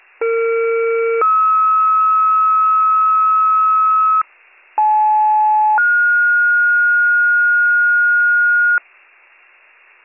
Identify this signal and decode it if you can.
Two-tone page, heard as a one-second tone, then a three-second tone, about 460 then 1290 Hz, about 860 then 1430 Hz, with noise. It is heard from 0.2 to 8.8 s.